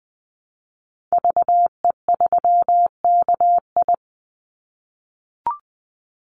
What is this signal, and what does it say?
Morse code, heard as a keyed tone, about 20 wpm, 700 Hz. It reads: VE3KI
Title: Call Signs - 20wpm